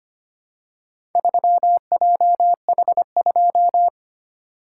Morse code, heard as a keyed tone, about 25 wpm, 690 Hz. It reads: 3JH2
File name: practice / 25wpm / Four Letter-Number Combinations - 25wpm